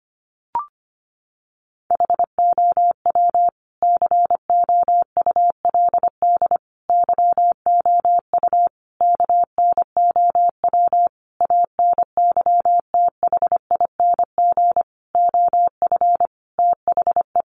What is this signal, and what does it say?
Morse code, heard as a keyed tone, about 25 wpm, 700 Hz. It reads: HOW COULD YOU KNOW ANYTHING OF THE
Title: Sentences from Top 1000 Words 25wpm